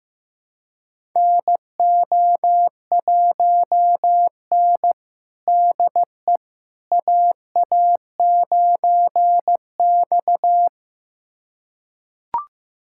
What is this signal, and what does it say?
Morse code, heard as a keyed tone, about 15 wpm, 700 Hz. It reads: NO1N DE AA9X
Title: USA Callsign DE Callsign - 4 Characters - 15wpm